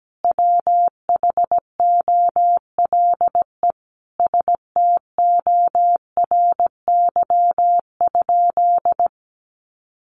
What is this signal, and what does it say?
Morse code, heard as a keyed tone, about 17 wpm, 700 Hz. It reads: WHOLE STORY?
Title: Sentences from Top 400 Words 17wpm